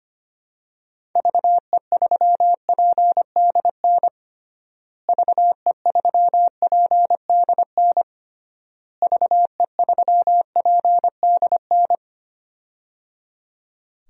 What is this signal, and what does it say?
Morse code, heard as a keyed tone, about 25 wpm, 700 Hz. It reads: VE3PDN VE3PDN VE3PDN
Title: DXCC-100 Call Signs - 6 Characters - Repeated 3x - 25wpm